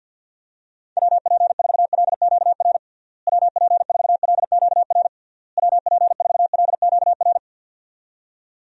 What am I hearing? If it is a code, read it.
Morse code, heard as a keyed tone, about 50 wpm, 700 Hz. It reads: WP4LQR WP4LQR WP4LQR
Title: USA Call Signs - 6 Characters - Repeated 3x - 50wpm